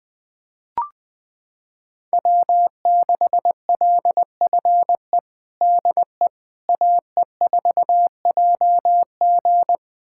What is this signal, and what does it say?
Morse code, heard as a keyed tone, about 20 wpm, 700 Hz. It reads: W6LFE DE AE4JG
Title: USA Callsign DE Callsign - 5 Characters - 20wpm